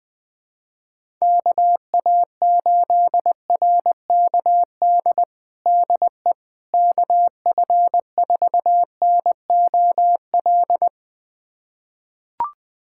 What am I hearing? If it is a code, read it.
Morse code, heard as a keyed tone, about 21 wpm, 700 Hz. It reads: KA8RKD DE KF4NOL
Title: USA Callsign DE Callsign - 6 Characters - 20wpm